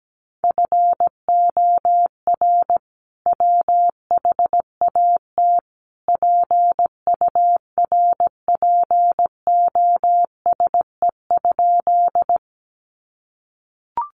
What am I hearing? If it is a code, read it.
Morse code, heard as a keyed tone, about 17 wpm, 700 Hz. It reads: FOR WHAT PURPOSE?
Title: Sentences from Top 800 Words 17wpm